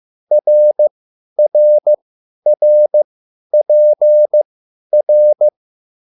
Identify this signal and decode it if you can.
Morse code, heard as a keyed tone, about 15 wpm, spaced as for 10 wpm, 600 Hz. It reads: RRRPR